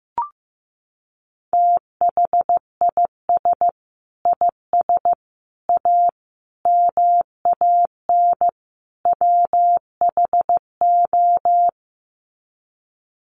Morse code, heard as a keyed tone, about 15 wpm, 700 Hz. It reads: THIS IS A MAN WHO
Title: Sets of 5 Words Encore2 15wpm